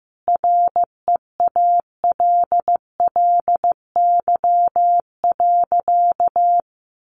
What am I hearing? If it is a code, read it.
Morse code, heard as a keyed tone, about 15 wpm, 700 Hz. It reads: REALLY.
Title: Sentences from Top 600 Words 15wpm